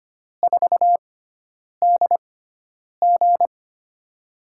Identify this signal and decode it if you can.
Morse code, heard as a keyed tone, about 25 wpm, spaced as for 9 wpm, 700 Hz. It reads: 4DG
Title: USA Call Signs - 4 Characters - 6x Character Spacing - 25wpm